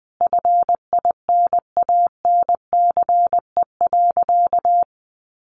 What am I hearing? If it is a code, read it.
Morse code, heard as a keyed tone, about 20 wpm, 700 Hz. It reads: FINANCE.
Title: Sentences from Top 1500 Words 20wpm